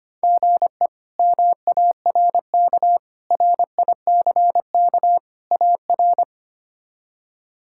Morse code, heard as a keyed tone, about 25 wpm, 700 Hz. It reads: GE MARK RICK AR
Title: K1USN SST QSO Practice - 25wpm